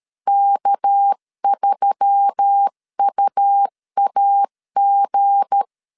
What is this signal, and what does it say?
Morse code, heard as a keyed tone, about 13 wpm, 800 Hz. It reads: K3UAG
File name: LEZIONI / 7 GF9Z / 7 G 12wpm 2055